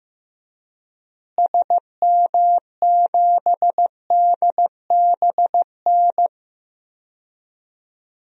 Morse code, heard as a keyed tone, about 15 wpm, 700 Hz. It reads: SM7DBN
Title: DXCC-100 Call Signs - 6 Characters - Repeated 3x - 15wpm